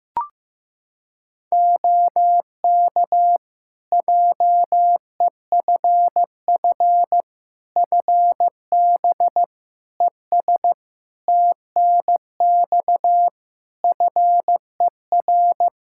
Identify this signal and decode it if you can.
Morse code, heard as a keyed tone, about 15 wpm, 700 Hz. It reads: OK JEFF FB ES TNX FER
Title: K1USN SST QSO Practice 15wpm